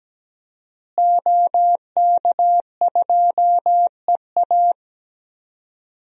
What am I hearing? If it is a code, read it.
Morse code, heard as a keyed tone, about 17 wpm, 700 Hz. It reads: OK2EA
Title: DXCC-100 Call Signs - 5 Characters - Repeated 2x - 17wpm